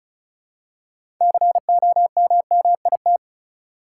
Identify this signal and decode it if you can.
Morse code, heard as a keyed tone, about 36 wpm, 700 Hz. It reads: COMMIT
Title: Six Letter Words - 35wpm